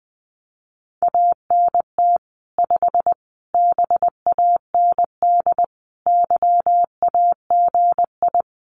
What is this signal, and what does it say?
Morse code, heard as a keyed tone, about 20 wpm, 700 Hz. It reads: ANT 5 BAND YAGI
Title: Common QSO Phrases 20wpm